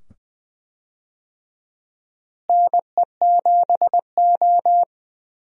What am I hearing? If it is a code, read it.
Morse code, heard as a keyed tone, about 20 wpm, 700 Hz. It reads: NE7O
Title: USA Call Signs - 4 Characters - Repeated 2x - 20wpm